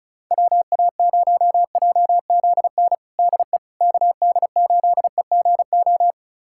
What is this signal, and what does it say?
Morse code, heard as a keyed tone, about 36 wpm, 700 Hz. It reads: WA0JZN DE KD8EGO
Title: USA Callsign DE Callsign 35wpm